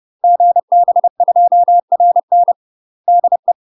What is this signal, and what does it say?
Morse code, heard as a keyed tone, about 30 wpm, 700 Hz. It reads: GB2RN DE
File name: UK MASTER QSOs TRAINING / 30WPM / UK MASTER QSOs 0018